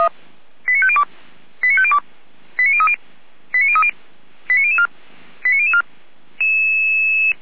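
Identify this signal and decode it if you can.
Five-tone selective call (ZVEI), heard as quick runs of five tones, tones of about 70 ms. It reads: 89502 89502 89030 89030 89004 89004
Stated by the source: ZVEI five-tone call received on SDR, from the Feuerwehr Orlamünde (fire brigade) audio server.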